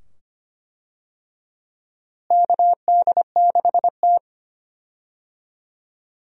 Morse code, heard as a keyed tone, about 25 wpm, 700 Hz. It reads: KD6T
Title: USA Call Signs - 4 Characters - Repeated 2x - 25wpm